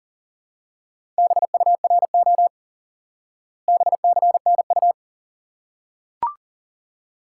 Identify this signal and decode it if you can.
Morse code, heard as a keyed tone, about 37 wpm, 700 Hz. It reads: BURO BCNU
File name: IQE Course 3 - Lesson 89 - YAGI - Two Elements - 40wpm